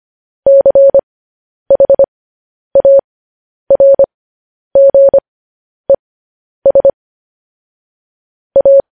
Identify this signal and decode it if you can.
Morse code, heard as a keyed tone, about 25 wpm, spaced as for 10 wpm, 550 Hz. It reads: CHARGES A